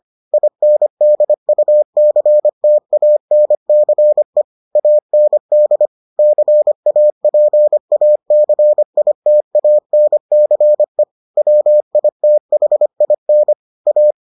Morse code, heard as a keyed tone, about 25 wpm, 600 Hz. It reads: INDUCTANCE AND CAPACITANCE WITHIN A